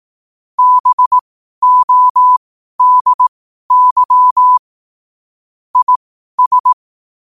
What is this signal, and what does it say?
Morse code, heard as a keyed tone, about 18 wpm, spaced as for 12 wpm, 1000 Hz. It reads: BODY IS
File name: qotdfiles / 12wpm